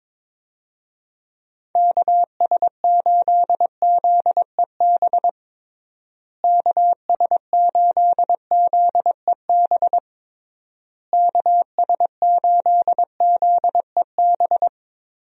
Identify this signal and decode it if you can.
Morse code, heard as a keyed tone, about 22 wpm, 700 Hz. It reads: KS8ZEB KS8ZEB KS8ZEB
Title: USA Call Signs - 6 Characters - Repeated 3x - 22wpm